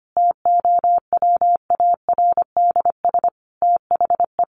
Morse code, heard as a keyed tone, about 25 wpm, 700 Hz. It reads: TOWARDS THE
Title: Additional Easy Phrases 25wpm